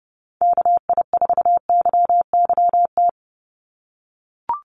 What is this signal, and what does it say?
Morse code, heard as a keyed tone, about 30 wpm, 700 Hz. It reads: KI4YYT
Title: USA Call Signs 30wpm